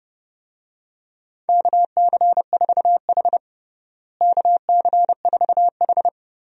Morse code, heard as a keyed tone, about 29 wpm, 700 Hz. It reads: KC4H KC4H
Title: USA Call Signs - 4 Characters - Repeated 2x - 30wpm